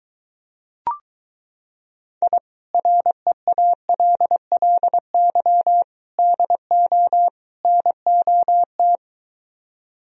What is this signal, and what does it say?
Morse code, heard as a keyed tone, about 23 wpm, 700 Hz. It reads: I REALLY DO NOT